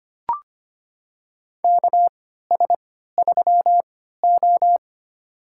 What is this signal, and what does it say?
Morse code, heard as a keyed tone, about 25 wpm, spaced as for 14 wpm, 700 Hz. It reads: KS3O
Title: USA Call Signs - 4 Characters - 3x Character Spacing - 25wpm